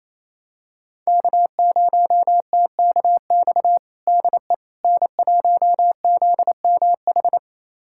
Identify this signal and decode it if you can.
Morse code, heard as a keyed tone, about 28 wpm, 700 Hz. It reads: K0TKX DE N1ZMH
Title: USA Callsign DE Callsign - 5 Characters - 28wpm